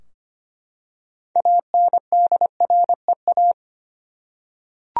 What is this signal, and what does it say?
Morse code, heard as a keyed tone, about 25 wpm, 700 Hz. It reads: ANDREA
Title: USA Common Names - 25wpm